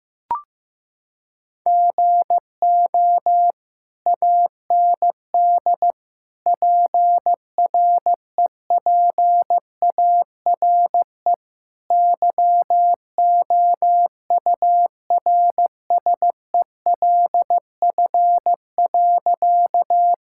Morse code, heard as a keyed tone, about 15 wpm, 700 Hz. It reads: GO AND PREPARE YOURSELF.
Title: Sentences from Top 1000 Words - Review Part 6 - 15wpm